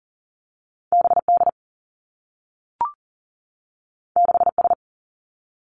Morse code, heard as a keyed tone, about 40 wpm, 700 Hz. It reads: B D 6 S
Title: Two Challenging Letter-Number Combinations 40wpm